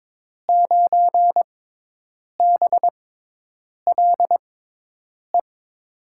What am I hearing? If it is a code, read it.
Morse code, heard as a keyed tone, about 22 wpm, spaced as for 8 wpm, 700 Hz. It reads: 9BLE
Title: USA Call Signs - 6 Characters - 6x Character Spacing - 22wpm